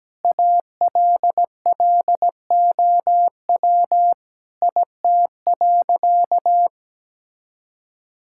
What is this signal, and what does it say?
Morse code, heard as a keyed tone, about 17 wpm, 700 Hz. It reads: ALLOW IT.
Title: Sentences from Top 1000 Words - Review Part 5 - 17wpm